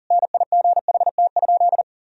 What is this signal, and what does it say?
Morse code, heard as a keyed tone, about 40 wpm, 700 Hz. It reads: NIGHT?
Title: Sentences from Top 1000 Words - Review Part 4 - 40wpm